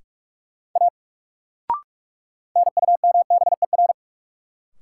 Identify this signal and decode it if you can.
Morse code, heard as a keyed tone, about 47 wpm, 700 Hz. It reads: A NUMBER
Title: Sentences from Top 100 Words - New Word Drill - 45wpm